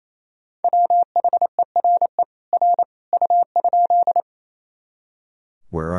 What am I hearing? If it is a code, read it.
Morse code, heard as a keyed tone, about 28 wpm, 700 Hz. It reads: WHERE R U?
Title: Sentences from Top 200 Words and Abbreviations 28wpm